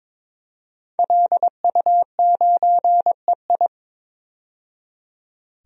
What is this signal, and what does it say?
Morse code, heard as a keyed tone, about 22 wpm, 700 Hz. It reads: LU9EI
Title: DXCC-100 Call Signs - 5 Characters - 22wpm